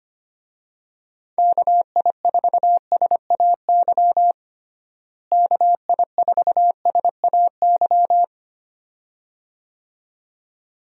Morse code, heard as a keyed tone, about 25 wpm, 700 Hz. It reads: KI4SAY KI4SAY
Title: USA Call Signs - 6 Characters - Repeated 2x - 25wpm